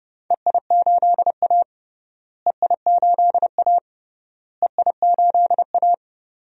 Morse code, heard as a keyed tone, about 30 wpm, 700 Hz. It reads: EI8A EI8A EI8A
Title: DXCC-100 Call Signs - 4 Characters - Repeated 3x - 30wpm